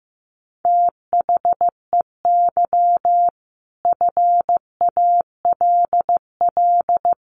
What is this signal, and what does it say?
Morse code, heard as a keyed tone, about 15 wpm, 700 Hz. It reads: THEY FALL
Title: Sentences from Top 1000 Words 15wpm